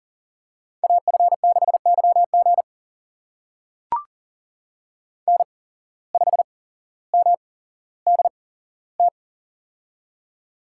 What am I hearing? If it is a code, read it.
Morse code, heard as a keyed tone, about 40 wpm, spaced as for 17 wpm, 700 Hz. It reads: AF6YG N5MDT
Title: USA Call Signs - 5 Characters - 8x Character Spacing - 40wpm